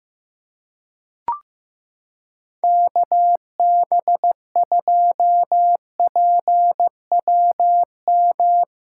Morse code, heard as a keyed tone, about 15 wpm, 700 Hz. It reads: KB2PWM
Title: USA Callsign DE Callsign - 6 Characters - 15wpm